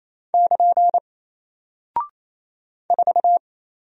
Morse code, heard as a keyed tone, about 28 wpm, 700 Hz. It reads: <KN> 4
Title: Single Character (Letter:Number:Punctuation:Prosign) Alt 28wpm